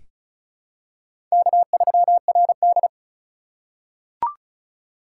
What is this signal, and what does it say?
Morse code, heard as a keyed tone, about 36 wpm, 700 Hz. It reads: K3RD
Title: USA Call Signs - 4 Characters - 3x Character Spacing - 35wpm